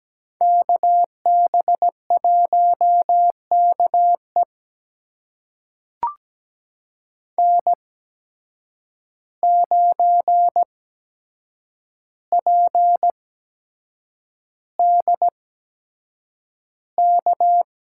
Morse code, heard as a keyed tone, about 17 wpm, spaced as for 7 wpm, 700 Hz. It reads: KB1KE N9PDK
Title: USA Call Signs - 5 Characters - 8x Character Spacing - 17wpm